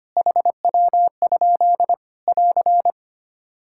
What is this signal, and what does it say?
Morse code, heard as a keyed tone, about 25 wpm, 700 Hz. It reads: HW? <AR>
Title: Practice QSO - Exchange 2 - Info - 25wpm